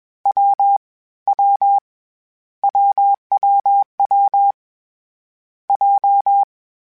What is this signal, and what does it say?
Morse code, heard as a keyed tone, about 21 wpm, 800 Hz. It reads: W W WWW J
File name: Difficult Combos / Difficult Combos-mp3s / AWJ @20